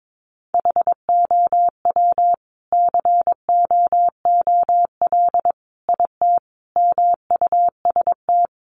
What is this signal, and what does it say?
Morse code, heard as a keyed tone, about 22 wpm, 700 Hz. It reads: HOW COOL IT MUST
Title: Sentences from Top 1200 Words 22wpm